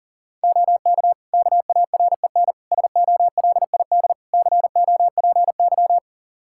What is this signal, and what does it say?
Morse code, heard as a keyed tone, about 40 wpm, 700 Hz. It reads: OK KAREN SOLID COPY
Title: Practice QSO - Exchange 3 - More Info - 40wpm